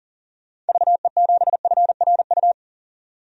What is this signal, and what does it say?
Morse code, heard as a keyed tone, about 40 wpm, 700 Hz. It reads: VE7FRU
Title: DXCC-100 Call Signs - 6 Characters - 40wpm